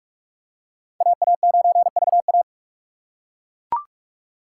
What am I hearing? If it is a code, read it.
Morse code, heard as a keyed tone, about 41 wpm, 700 Hz. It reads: AA9VA